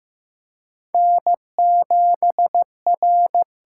Morse code, heard as a keyed tone, about 15 wpm, 700 Hz. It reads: N7R
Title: USA Call Signs - 5 Characters - 8x Character Spacing - 15wpm